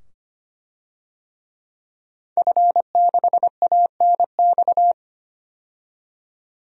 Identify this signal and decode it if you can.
Morse code, heard as a keyed tone, about 25 wpm, 700 Hz. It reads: F6ANX
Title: DXCC-100 Call Signs - 5 Characters - 25wpm